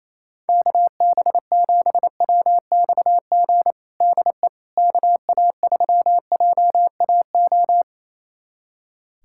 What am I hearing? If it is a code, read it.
Morse code, heard as a keyed tone, about 28 wpm, 700 Hz. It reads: KB7WXG DE KA3JAO
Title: USA Callsign DE Callsign 28wpm